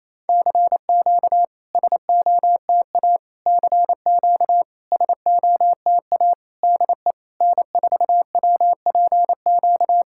Morse code, heard as a keyed tone, about 28 wpm, 700 Hz. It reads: CQ SOTA CQ SOTA DE N4WPQ
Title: SOTA QSO Practice 28wpm